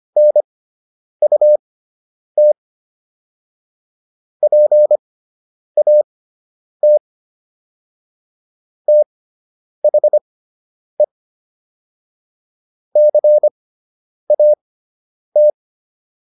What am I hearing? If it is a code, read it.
Morse code, heard as a keyed tone, about 25 wpm, spaced as for 9 wpm, 600 Hz. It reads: NUT PAT THE CAT